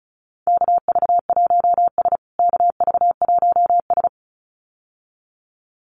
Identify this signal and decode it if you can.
Morse code, heard as a keyed tone, about 34 wpm, 700 Hz. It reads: KV1S KV1S
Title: USA Call Signs Repeated 2x 35wpm